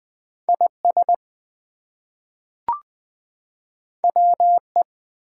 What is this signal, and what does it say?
Morse code, heard as a keyed tone, about 20 wpm, 700 Hz. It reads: IS WE
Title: Two Letter Words - 20wpm